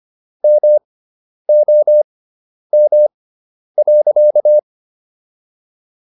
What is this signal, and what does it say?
Morse code, heard as a keyed tone, about 25 wpm, spaced as for 10 wpm, 600 Hz. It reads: MOM.